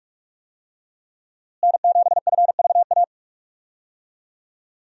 Morse code, heard as a keyed tone, about 46 wpm, 700 Hz. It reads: N7FVA
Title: USA Call Signs - 5 Characters - Repeated 2x - 45wpm